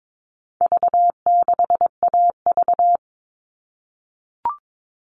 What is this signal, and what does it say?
Morse code, heard as a keyed tone, about 22 wpm, 700 Hz. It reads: V6AV
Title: All Cut Numbers - 4-Digit 22wpm